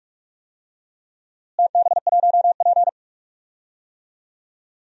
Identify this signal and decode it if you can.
Morse code, heard as a keyed tone, about 44 wpm, 700 Hz. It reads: TB1P
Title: DXCC-100 Call Signs - 4 Characters - Repeated 3x - 45wpm